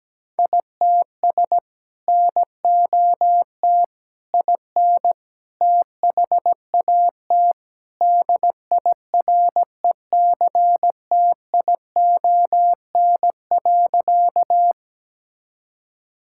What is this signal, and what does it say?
Morse code, heard as a keyed tone, about 17 wpm, 700 Hz. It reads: ITS NOT IN THAT DIRECTION.
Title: Sentences from Top 1000 Words - Review Part 5 - 17wpm